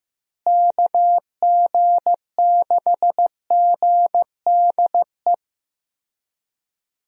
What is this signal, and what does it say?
Morse code, heard as a keyed tone, about 15 wpm, 700 Hz. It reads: KG6GDE
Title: USA Call Signs - 6 Characters - Repeated 2x - 15wpm